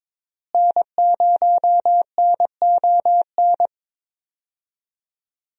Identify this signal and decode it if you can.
Morse code, heard as a keyed tone, about 22 wpm, 700 Hz. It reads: N0NON
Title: USA Call Signs - 5 Characters - Repeated 3x - 22wpm